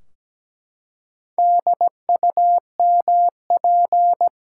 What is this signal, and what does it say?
Morse code, heard as a keyed tone, about 17 wpm, 700 Hz. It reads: DUMP
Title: Four Letter Words - 17wpm